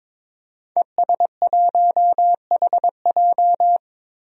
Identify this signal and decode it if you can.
Morse code, heard as a keyed tone, about 22 wpm, 700 Hz. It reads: ES1HJ
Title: DXCC-100 Call Signs - 5 Characters - Repeated 3x - 22wpm